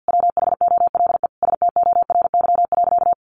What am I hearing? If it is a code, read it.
Morse code, heard as a keyed tone, about 50 wpm, 700 Hz. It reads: WHOLE STORY.
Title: Sentences from Top 1100 Words 50wpm